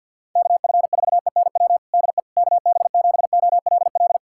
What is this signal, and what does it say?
Morse code, heard as a keyed tone, about 52 wpm, 700 Hz. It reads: KF4ENW DE KB7OLL
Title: USA Callsign DE Callsign - 6 Characters - 50wpm